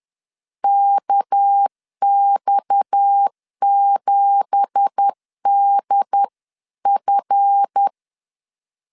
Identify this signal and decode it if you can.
Morse code, heard as a keyed tone, about 10 wpm, 800 Hz. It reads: KX7DF